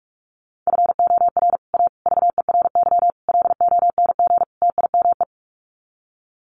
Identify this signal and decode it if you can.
Morse code, heard as a keyed tone, about 44 wpm, 700 Hz. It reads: FOR A VERY LONG TIME
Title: Sets of 5 Words 45wpm